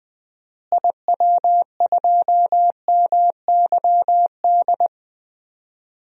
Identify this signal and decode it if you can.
Morse code, heard as a keyed tone, about 20 wpm, 700 Hz. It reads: IW2MYD
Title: DXCC-100 Call Signs - 6 Characters - Repeated 2x - 20wpm